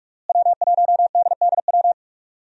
Morse code, heard as a keyed tone, about 46 wpm, 700 Hz. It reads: W1DDW
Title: USA Call Signs - 5 Characters - Repeated 2x - 45wpm